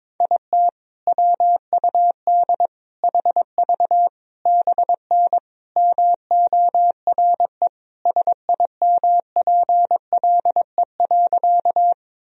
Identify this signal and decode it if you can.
Morse code, heard as a keyed tone, about 22 wpm, 700 Hz. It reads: IT WUD HV BN MORE SIMPLE.
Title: Sentences from Top 400 Words and Abbreviations 22wpm